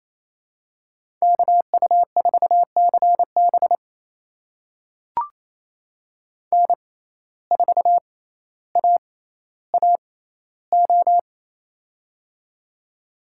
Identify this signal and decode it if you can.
Morse code, heard as a keyed tone, about 28 wpm, spaced as for 14 wpm, 700 Hz. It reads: KU4CB N4AAO
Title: USA Call Signs - 5 Characters - 6x Character Spacing - 28wpm